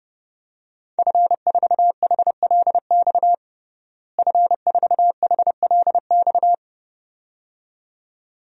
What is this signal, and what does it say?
Morse code, heard as a keyed tone, about 30 wpm, 700 Hz. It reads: F4HLX F4HLX
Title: DXCC-100 Call Signs - 5 Characters - Repeated 2x - 30wpm